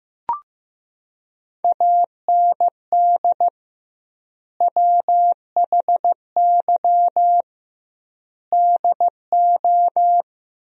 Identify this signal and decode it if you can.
Morse code, heard as a keyed tone, about 16 wpm, 700 Hz. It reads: AND WHY DO
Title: 2x Word-Spacing - Sentences from Top 400 Words - 15wpm